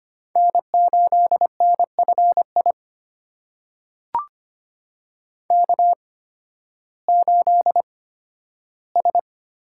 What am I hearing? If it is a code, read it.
Morse code, heard as a keyed tone, about 25 wpm, spaced as for 13 wpm, 700 Hz. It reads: N8NFI K8S
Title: USA Call Signs - 5 Characters - 8x Character Spacing - 25wpm